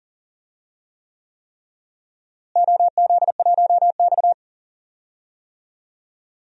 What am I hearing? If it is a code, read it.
Morse code, heard as a keyed tone, about 40 wpm, 700 Hz. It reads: OZ1X
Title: DXCC-100 Call Signs - 4 Characters - Repeated 3x - 40wpm